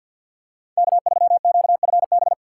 Morse code, heard as a keyed tone, about 49 wpm, 700 Hz. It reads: K3QFB
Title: USA Call Signs - 5 Characters - 3x Character Spacing - 50wpm